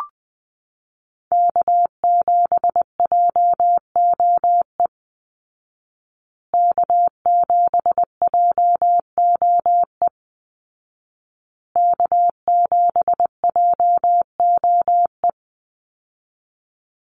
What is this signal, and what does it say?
Morse code, heard as a keyed tone, about 20 wpm, 700 Hz. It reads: K7JOE K7JOE K7JOE
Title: Call Signs Repeated 2x - 20wpm